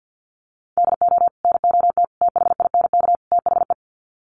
Morse code, heard as a keyed tone, about 50 wpm, 700 Hz. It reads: DO NOT THINK THE
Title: Sets of 4 Words 50wpm